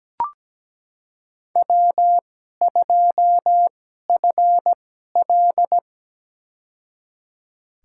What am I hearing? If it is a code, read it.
Morse code, heard as a keyed tone, about 17 wpm, spaced as for 12 wpm, 700 Hz. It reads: W2FL
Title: USA Call Signs - 4 Characters - 2x Character Spacing - 17wpm